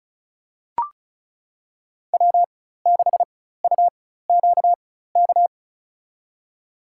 Morse code, heard as a keyed tone, about 36 wpm, spaced as for 16 wpm, 700 Hz. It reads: W6UQK
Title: USA Call Signs - 4x Character Spacing - 35wpm